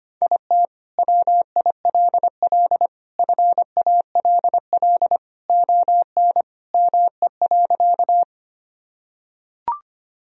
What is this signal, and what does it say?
Morse code, heard as a keyed tone, about 25 wpm, 700 Hz. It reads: IT WILL FALL ON ME.
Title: Sentences from Top 1000 Words - Review Part 2 - 25wpm